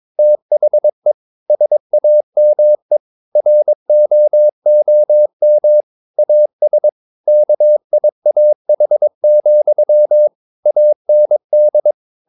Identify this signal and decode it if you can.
Morse code, heard as a keyed tone, about 22 wpm, 600 Hz. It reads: THE SAME ROOM AS KIAH, AND
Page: Romanen Never af Ken Follet på engelsk.